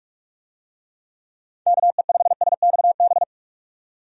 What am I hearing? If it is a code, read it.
Morse code, heard as a keyed tone, about 46 wpm, 700 Hz. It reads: KE5SXB
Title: USA Call Signs - 6 Characters - 45wpm